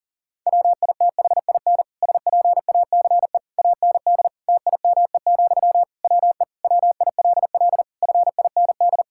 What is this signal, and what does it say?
Morse code, heard as a keyed tone, about 40 wpm, 700 Hz. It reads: WITHIN SPACE AND TIME, WE WILL FIND
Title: Sentences from Top 1000 Words - Review Part 5 - 40wpm